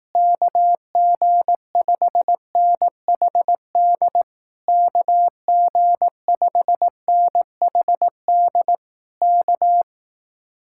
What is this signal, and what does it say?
Morse code, heard as a keyed tone, about 18 wpm, 700 Hz. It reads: KG5NHD KG5NHD K
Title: Speed-Racing Common QSO Phrases Encore - 15wpm